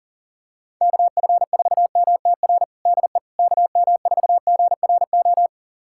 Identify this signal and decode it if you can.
Morse code, heard as a keyed tone, about 37 wpm, 700 Hz. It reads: KF4MTR DE KM4GRO
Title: USA Callsign DE Callsign 40wpm